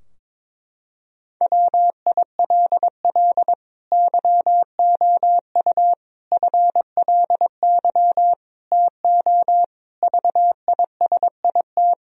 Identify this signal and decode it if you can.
Morse code, heard as a keyed tone, about 22 wpm, 700 Hz. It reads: WILL YOU FLY TO VISIT
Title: Sentences from Top 1000 Words - Review Part 2 - 22wpm